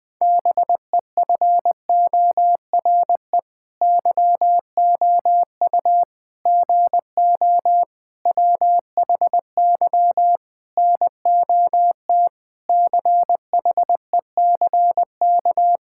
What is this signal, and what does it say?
Morse code, heard as a keyed tone, about 20 wpm, 700 Hz. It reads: BEFORE YOU GO WHY NOT CHECK
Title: Sentences with 9-Word Phrases - Spoken Beginning - 20wpm